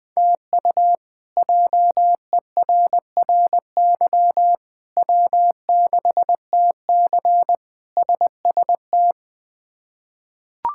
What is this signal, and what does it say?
Morse code, heard as a keyed tone, about 21 wpm, 700 Hz. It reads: TU JERRY W6TC SST
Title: K1USN SST QSO Practice - 20wpm